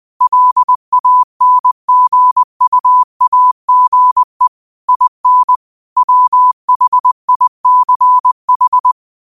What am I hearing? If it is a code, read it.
Morse code, heard as a keyed tone, about 20 wpm, 1000 Hz. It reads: LANGUAGE IN WHICH